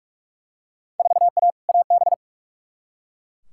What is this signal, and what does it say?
Morse code, heard as a keyed tone, about 42 wpm, 700 Hz. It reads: 4A AB
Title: Field Day Exchanges Encore - 45wpm